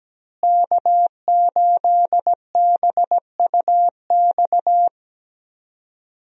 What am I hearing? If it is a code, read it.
Morse code, heard as a keyed tone, about 17 wpm, 700 Hz. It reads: K8BUX
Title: USA Call Signs - 5 Characters - Repeated 3x - 17wpm